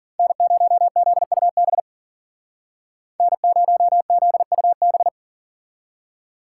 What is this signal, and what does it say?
Morse code, heard as a keyed tone, about 51 wpm, 700 Hz. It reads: N0ZUB N0ZUB